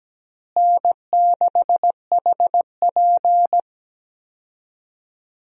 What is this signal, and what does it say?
Morse code, heard as a keyed tone, about 17 wpm, 700 Hz. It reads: N6HP
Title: USA Call Signs - 4 Characters - Repeated 2x - 17wpm